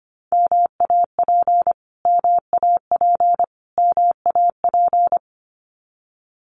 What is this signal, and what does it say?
Morse code, heard as a keyed tone, about 25 wpm, 700 Hz. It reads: MAP MAP MAP
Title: Top 500 Words Repeated 25wpm